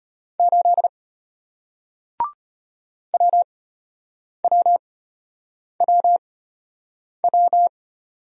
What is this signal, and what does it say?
Morse code, heard as a keyed tone, about 34 wpm, spaced as for 8 wpm, 700 Hz. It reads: 8 WWWW